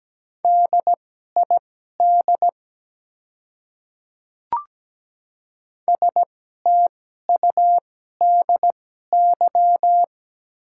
Morse code, heard as a keyed tone, about 17 wpm, spaced as for 12 wpm, 700 Hz. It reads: DID STUDY
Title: 2x Character Spacing - Top 300 Words - 17wpm